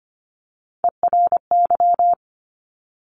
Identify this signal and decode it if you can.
Morse code, heard as a keyed tone, about 25 wpm, 700 Hz. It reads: ERY
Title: Top Three Letter Combinations in Words 25wpm